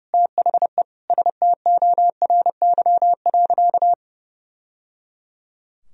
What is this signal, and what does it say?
Morse code, heard as a keyed tone, about 30 wpm, 700 Hz. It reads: THE STORY.
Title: Sentences from Top 1000 Words - Review Part 2 - 30wpm